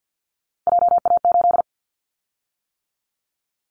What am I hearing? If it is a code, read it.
Morse code, heard as a keyed tone, about 50 wpm, 700 Hz. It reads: JA8
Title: Callsign Prefixes for Top 50 DXCC - 50wpm